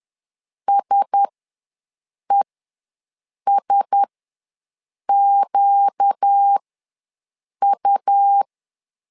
Morse code, heard as a keyed tone, about 10 wpm, spaced as for 6 wpm, 800 Hz. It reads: SESQU
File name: LEZIONI / 4 Q1S ws 10wpm 2208